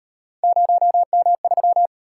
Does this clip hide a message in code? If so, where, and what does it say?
Yes: Morse code from 0.4 to 1.9 s, heard as a keyed tone, about 36 wpm, 700 Hz. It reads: 0M3